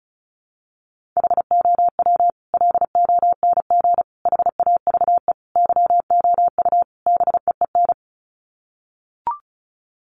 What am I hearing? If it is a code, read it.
Morse code, heard as a keyed tone, about 34 wpm, 700 Hz. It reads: HOW LONG HAVE YOU BEEN
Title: Sets of 5 Words Encore 35wpm